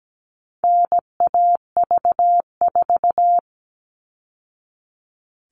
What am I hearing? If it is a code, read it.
Morse code, heard as a keyed tone, about 17 wpm, 700 Hz. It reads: NAV4
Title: All Cut Numbers - 4-Digit 17wpm